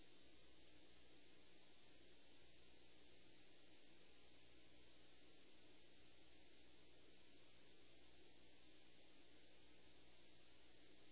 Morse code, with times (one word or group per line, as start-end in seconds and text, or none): none